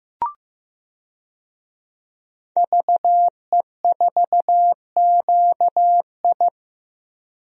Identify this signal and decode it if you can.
Morse code, heard as a keyed tone, about 15 wpm, 700 Hz. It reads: VE4QI
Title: DXCC-100 Call Signs - 5 Characters - Repeated 3x - 15wpm